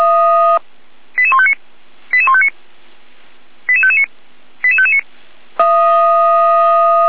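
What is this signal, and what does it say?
Five-tone selective call (ZVEI), heard as quick runs of five tones, tones of about 70 ms. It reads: 88159 88159 88559 88559
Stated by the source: Five-tone ZVEI selective call received by SDR, from the fire brigade Feuerwehr Orlamünde.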